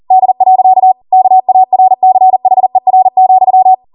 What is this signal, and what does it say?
Morse code, heard as a keyed tone, about 40 wpm, 750 Hz. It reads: D. KARCHER,